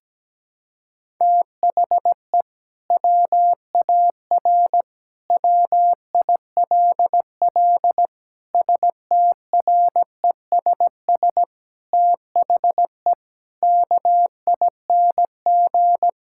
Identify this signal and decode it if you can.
Morse code, heard as a keyed tone, about 17 wpm, 700 Hz. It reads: THE WAR WILL STRESS THE KING
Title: Sentences from Top 1000 Words - Review Part 6 - 17wpm